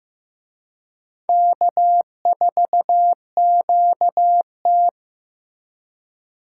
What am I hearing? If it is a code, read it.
Morse code, heard as a keyed tone, about 18 wpm, 700 Hz. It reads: K4QT
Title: Speed Racing - USA 1x2 Call Signs 15wpm